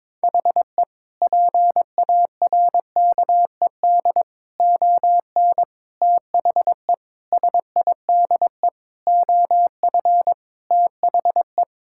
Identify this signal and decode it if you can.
Morse code, heard as a keyed tone, about 22 wpm, 700 Hz. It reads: HE PARKED ON THE SIDE OF THE
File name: practice / 22wpm / Sentences with 4-Word Phrases - 22wpm